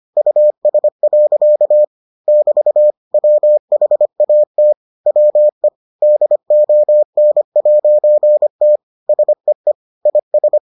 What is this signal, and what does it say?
Morse code, heard as a keyed tone, about 25 wpm, 600 Hz. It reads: US. <BT> WHAT WE DON'T SEE IS